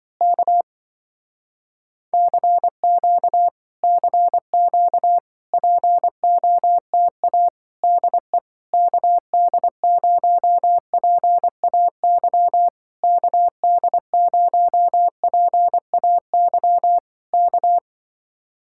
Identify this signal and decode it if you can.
Morse code, heard as a keyed tone, about 24 wpm, 700 Hz. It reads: K CQ CQ POTA DE KD0PAY KD0PAY K
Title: Speed-Racing Common QSO Phrases - 20wpm